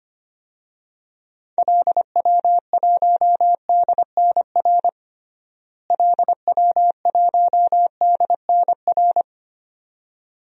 Morse code, heard as a keyed tone, about 25 wpm, 700 Hz. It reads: LW1DNR LW1DNR
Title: DXCC-100 Call Signs - 6 Characters - Repeated 2x - 25wpm